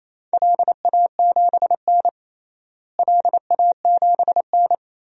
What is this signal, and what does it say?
Morse code, heard as a keyed tone, about 28 wpm, 700 Hz. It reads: LA7N LA7N
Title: DXCC-100 Call Signs - 4 Characters - Repeated 3x - 28wpm